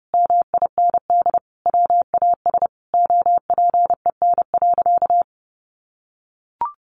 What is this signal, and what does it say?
Morse code, heard as a keyed tone, about 30 wpm, 700 Hz. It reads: MIND WAS OPEN.
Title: Sentences from Top 1200 Words 30wpm